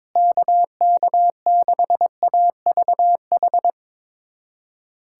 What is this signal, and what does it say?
Morse code, heard as a keyed tone, about 22 wpm, 700 Hz. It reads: KK6AVH
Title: USA Callsign DE Callsign - 6 Characters - 22wpm